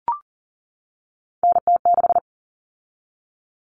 Morse code, heard as a keyed tone, about 40 wpm, 700 Hz. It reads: NT6
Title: All Cut Numbers - 3-Digit 40wpm